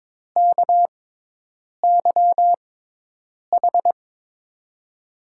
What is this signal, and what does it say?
Morse code, heard as a keyed tone, about 22 wpm, spaced as for 8 wpm, 700 Hz. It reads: KYH
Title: USA Call Signs - 6x Character Spacing - 22wpm